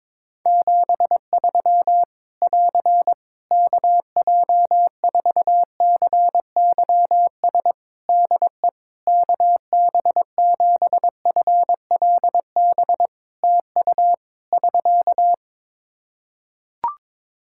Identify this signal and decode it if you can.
Morse code, heard as a keyed tone, about 22 wpm, 700 Hz. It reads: 73 <AR> KJ4CYS DE KB7FLB TU <SK>
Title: Practice QSO - Exchange 4 - Ending - 22wpm